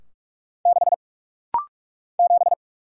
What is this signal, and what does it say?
Morse code, heard as a keyed tone, about 46 wpm, 700 Hz. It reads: 6 7
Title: Single Letter-Number - Rapid-Fire 45wpm